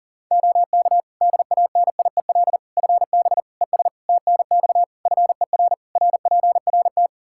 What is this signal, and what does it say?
Morse code, heard as a keyed tone, about 40 wpm, 700 Hz. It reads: OK DANIEL FB ES TNX FER RPRT
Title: Practice QSO - Exchange 2 - Info - 40wpm